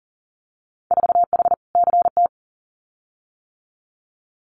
Morse code, heard as a keyed tone, about 41 wpm, 700 Hz. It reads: 4H CT
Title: Winter Field Day Exchanges Encore 40wpm